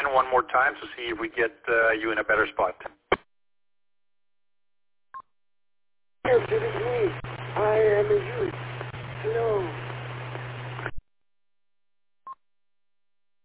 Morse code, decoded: none

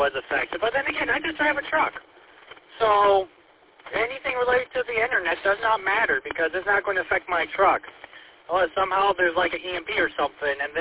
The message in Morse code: none

